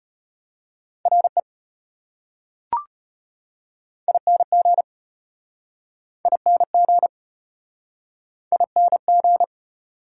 Morse code, heard as a keyed tone, about 34 wpm, 700 Hz. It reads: RE ING ING ING